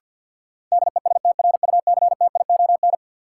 Morse code, heard as a keyed tone, about 47 wpm, 700 Hz. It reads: DESTRUCTION
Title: Sets of 5 Words Encore - New Word Drill - 50wpm